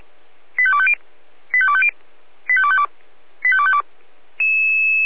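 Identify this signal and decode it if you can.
Five-tone selective call (ZVEI), heard as quick runs of five tones, tones of about 70 ms. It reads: 85250 85250 85252 85252
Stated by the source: ZVEI five-tone call 85302 received by SDR.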